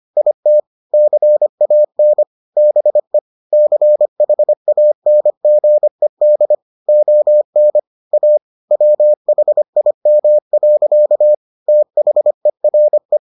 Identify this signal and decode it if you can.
Morse code, heard as a keyed tone, about 25 wpm, 600 Hz. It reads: IT CAN BE CHANGED ON A WHIM. THERE